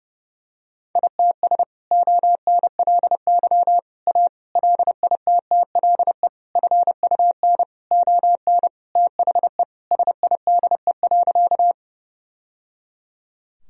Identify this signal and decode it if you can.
Morse code, heard as a keyed tone, about 30 wpm, 700 Hz. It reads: ITS ONLY A LITTLE FUN ON THE SIDE.
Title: Sentences from Top 1000 Words - Review Part 4 - 30wpm